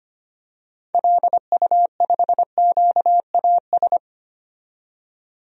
Morse code, heard as a keyed tone, about 25 wpm, 700 Hz. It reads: LU5QAS
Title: DXCC-100 Call Signs - 6 Characters - 25wpm